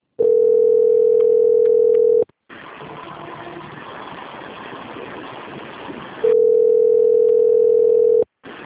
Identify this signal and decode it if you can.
Ringback tone (North America), 440 + 480 Hz.